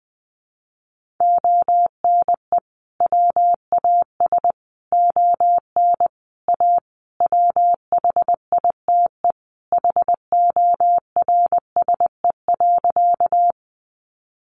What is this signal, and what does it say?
Morse code, heard as a keyed tone, about 20 wpm, 700 Hz. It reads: ONE WAS ON A WHITE HORSE.
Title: Sentences from Top 300 Words 20wpm